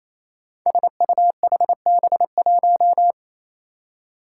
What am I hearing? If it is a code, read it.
Morse code, heard as a keyed tone, about 28 wpm, 700 Hz. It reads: SUHB1
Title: Five Challenging Letter-Number Combinations 28wpm